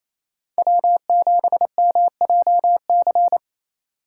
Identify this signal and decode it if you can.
Morse code, heard as a keyed tone, about 28 wpm, 700 Hz. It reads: W7MJC
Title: USA Call Signs - 5 Characters - 6x Character Spacing - 28wpm